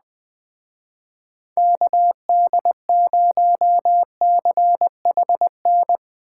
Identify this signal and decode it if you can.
Morse code, heard as a keyed tone, about 20 wpm, 700 Hz. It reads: KD0CHN
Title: Speed Racing - USA 2x3 Call Signs 17wpm